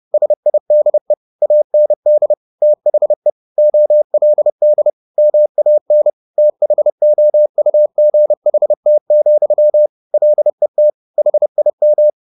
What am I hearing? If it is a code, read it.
Morse code, heard as a keyed tone, about 30 wpm, 600 Hz. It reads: SIDE AND THE OLD MAN THOUGHT, LET HIM